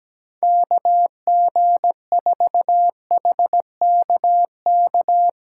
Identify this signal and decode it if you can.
Morse code, heard as a keyed tone, about 18 wpm, 700 Hz. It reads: KG4HKK
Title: Speed Racing - USA 2x3 Call Signs 17wpm